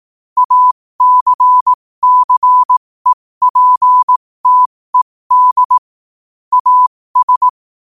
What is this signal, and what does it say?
Morse code, heard as a keyed tone, about 18 wpm, spaced as for 15 wpm, 1000 Hz. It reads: ACCEPTED AS
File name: qotdfiles / 15wpm